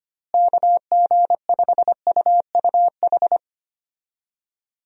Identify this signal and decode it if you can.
Morse code, heard as a keyed tone, about 25 wpm, 700 Hz. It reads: KG5UUH
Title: USA Call Signs - 6 Characters - Repeated 3x - 25wpm